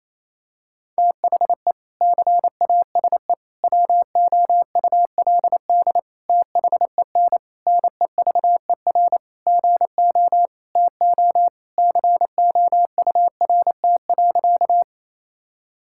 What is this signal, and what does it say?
Morse code, heard as a keyed tone, about 28 wpm, 700 Hz. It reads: THE CASE WOULD THEN NEVER GO TO COURT.
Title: Sentences from Top 1600 Words 28wpm